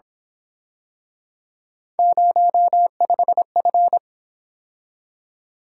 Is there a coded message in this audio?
Yes: Morse code from 2.0 to 4.0 s, heard as a keyed tone, about 28 wpm, 700 Hz. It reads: 05F